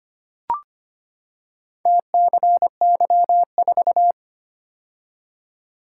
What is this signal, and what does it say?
Morse code, heard as a keyed tone, about 25 wpm, 700 Hz. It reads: TCY4
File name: practice / 25wpm / Four Letter-Number Combinations - 25wpm